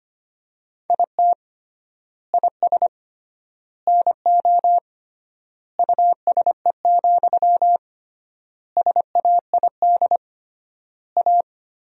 Morse code, heard as a keyed tone, about 25 wpm, 700 Hz. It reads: IT IS NO USE, SAID A
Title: 3x Word-Spacing - Sentences from Top 500 Words - 25wpm